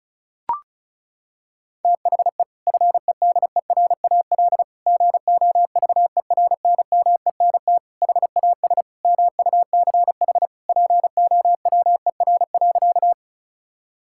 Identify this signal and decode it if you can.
Morse code, heard as a keyed tone, about 36 wpm, 690 Hz. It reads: THE FEDERAL GOVERNMENT HAS MUCH POWER.
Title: Sentences from Top 1000 Words - Review Part 5 - 35wpm